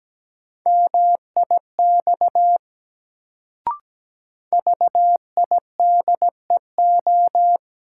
Morse code, heard as a keyed tone, about 17 wpm, 700 Hz. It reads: MIX VIDEO
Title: Sentences from Top 1000 Words - Review Part 4 - New Word Drill - 17wpm